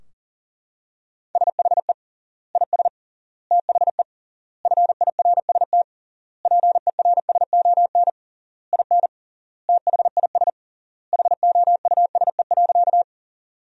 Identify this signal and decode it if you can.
Morse code, heard as a keyed tone, about 37 wpm, 700 Hz. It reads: SHE IS THE FIRST PERSON IN THIS HOUSE.
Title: 3x Word-Spacing - Sentences from Top 400 Words - 40wpm